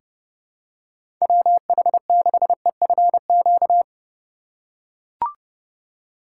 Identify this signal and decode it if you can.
Morse code, heard as a keyed tone, about 30 wpm, 700 Hz. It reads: WH6EFQ
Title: USA Call Signs - 6 Characters - 2x Character Spacing - 30wpm